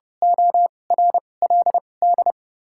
Morse code, heard as a keyed tone, about 32 wpm, spaced as for 22 wpm, 700 Hz. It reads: ORLD
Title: 2x Character Spacing - Top 300 Words - 30wpm